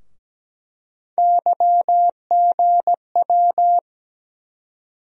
Morse code, heard as a keyed tone, about 17 wpm, 700 Hz. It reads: YGW
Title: USA Call Sign Postfixes - 3 Letters 17wpm